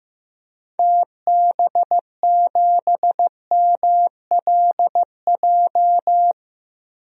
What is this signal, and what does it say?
Morse code, heard as a keyed tone, about 15 wpm, 700 Hz. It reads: TB7MLJ
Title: DXCC-100 Call Signs - 6 Characters - 15wpm